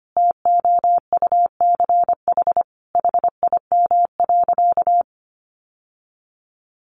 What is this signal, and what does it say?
Morse code, heard as a keyed tone, about 25 wpm, 700 Hz. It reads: TOUCH HIM.
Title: Sentences from Top 1200 Words 25wpm